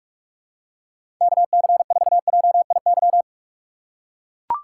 Morse code, heard as a keyed tone, about 44 wpm, 700 Hz. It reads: KC4JIY
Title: USA Call Signs - 6 Characters - 6x Character Spacing - 45wpm